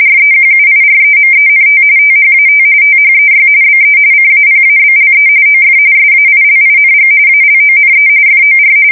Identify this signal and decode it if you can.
RTTY, about 45.45 baud. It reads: REQUIREMENT THAT THEY BE CAPABLE OF SAILING ALONE FOR
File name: fsk170 / usnavy4